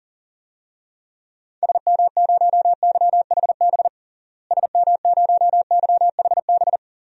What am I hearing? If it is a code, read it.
Morse code, heard as a keyed tone, about 40 wpm, 700 Hz. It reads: SM0YHB SM0YHB
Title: DXCC-100 Call Signs - 6 Characters - Repeated 2x - 40wpm